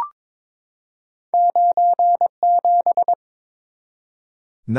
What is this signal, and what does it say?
Morse code, heard as a keyed tone, about 22 wpm, 700 Hz. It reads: 97
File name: practice / 22wpm / Two Letter-Number Combinations - 22wpm